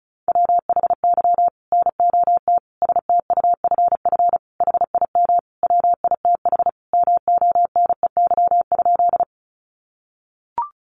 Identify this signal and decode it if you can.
Morse code, heard as a keyed tone, about 34 wpm, 700 Hz. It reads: WHY NOT STUFF HIM WITH MONEY?
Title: Sentences from Top 1400 Words 35wpm